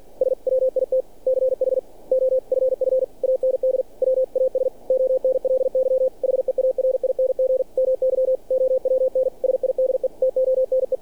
Morse code, heard as a keyed tone, about 48 wpm, 520 Hz. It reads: SWITCH OFF AND WAS ONLY HEARING MY OWN SIDE TONE
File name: archived / CRSnet